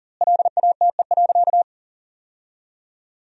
Morse code, heard as a keyed tone, about 40 wpm, 700 Hz. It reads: LATE.
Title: 2x Word-Spacing - Sentences from Top 300 Words - 40wpm